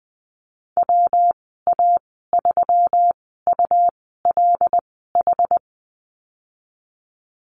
Morse code, heard as a keyed tone, about 19 wpm, spaced as for 14 wpm, 700 Hz. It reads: WA3ULH
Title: 128 Lesson 21g - Number 6 - Call Signs 20wpm